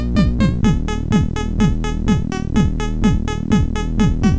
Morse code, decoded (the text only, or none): none